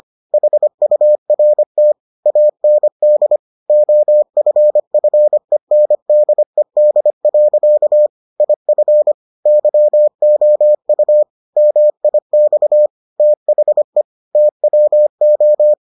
Morse code, heard as a keyed tone, about 25 wpm, 600 Hz. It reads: HURT AND OFFENDED. IF YOU MIX THE TWO